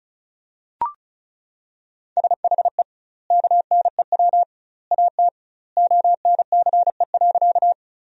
Morse code, heard as a keyed tone, about 36 wpm, 700 Hz. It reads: SHE KNEW AT ONCE.
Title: 2x Word-Spacing - Sentences from Top 400 Words - 35wpm